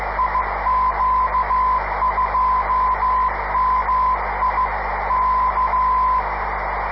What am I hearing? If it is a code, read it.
Morse code, heard as a keyed tone, about 15 wpm, 1020 Hz, with noise. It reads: IQ2MI K
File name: 500khz / 30 december 2011 / iq2mi qso-1